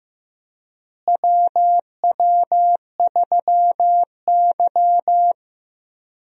Morse code, heard as a keyed tone, about 15 wpm, 700 Hz. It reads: WW3Y
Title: USA Call Signs - 4 Characters - Repeated 3x - 15wpm